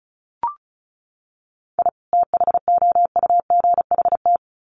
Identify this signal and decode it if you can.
Morse code, heard as a keyed tone, about 34 wpm, 700 Hz. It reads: I THOUGHT
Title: Sets of 2 Words 35wpm